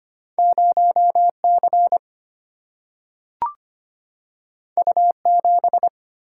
Morse code, heard as a keyed tone, about 25 wpm, 700 Hz. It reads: 0C U7
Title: Single-Letter Single-Number 25wpm